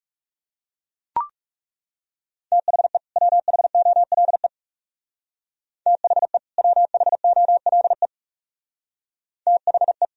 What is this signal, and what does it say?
Morse code, heard as a keyed tone, about 36 wpm, 700 Hz. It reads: THE WHOLE THE WHOLE THE
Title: Speed-Racing - Sets of 2 Words 30wpm